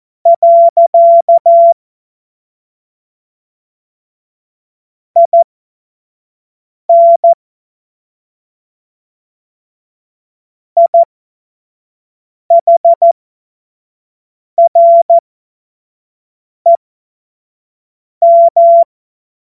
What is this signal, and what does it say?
Morse code, heard as a keyed tone, about 14 wpm, spaced as for 5 wpm, 680 Hz. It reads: . IN IHREM